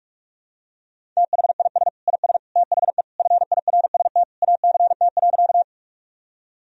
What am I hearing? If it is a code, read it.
Morse code, heard as a keyed tone, about 46 wpm, 700 Hz. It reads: THIS IS THE FIRST ACT.
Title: Sentences from Top 1000 Words - Review Part 1 - 45wpm